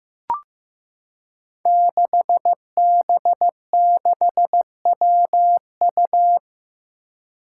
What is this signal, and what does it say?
Morse code, heard as a keyed tone, about 15 wpm, 700 Hz. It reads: 6B6WU
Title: Five Challenging Letter-Number Combinations 15wpm